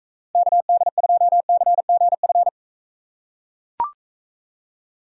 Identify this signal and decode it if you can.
Morse code, heard as a keyed tone, about 40 wpm, 700 Hz. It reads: KD2CGF